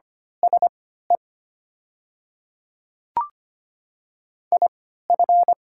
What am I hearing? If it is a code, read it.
Morse code, heard as a keyed tone, about 25 wpm, spaced as for 14 wpm, 700 Hz. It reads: SE IF
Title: 3x Character Spacing - Top 100 Words - 25wpm